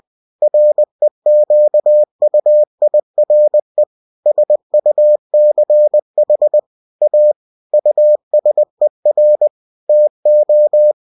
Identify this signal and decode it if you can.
Morse code, heard as a keyed tone, about 20 wpm, 600 Hz. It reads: REQUIRE SUCH A USER TO
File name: cw-books / winnie-the-pooh / winnie-20wpm-page-0117